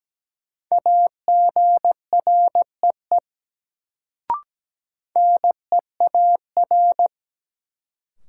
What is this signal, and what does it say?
Morse code, heard as a keyed tone, about 17 wpm, 700 Hz. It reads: AGREE NEAR
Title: Sets of 5 Words Encore - New Word Drill - 17wpm